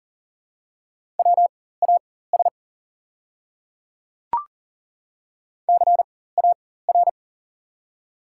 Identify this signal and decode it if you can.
Morse code, heard as a keyed tone, about 40 wpm, spaced as for 19 wpm, 700 Hz. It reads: WAS CAR
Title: 4x Character Spacing - Top 300 Words - 40wpm